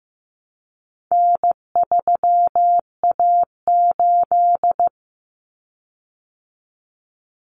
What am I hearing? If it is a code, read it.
Morse code, heard as a keyed tone, about 15 wpm, 700 Hz. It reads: N3A8
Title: Common Cut Numbers - 4-Digit 15wpm